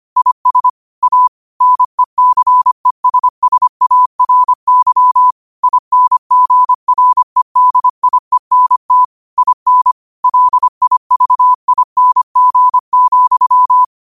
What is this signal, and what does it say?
Morse code, heard as a keyed tone, about 25 wpm, 1000 Hz. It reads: IS A NECESSARY INGREDIENT IN LIVING,